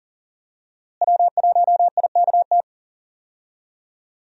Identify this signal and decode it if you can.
Morse code, heard as a keyed tone, about 37 wpm, 700 Hz. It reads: W1IKT
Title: USA Call Signs - 5 Characters - Repeated 2x - 40wpm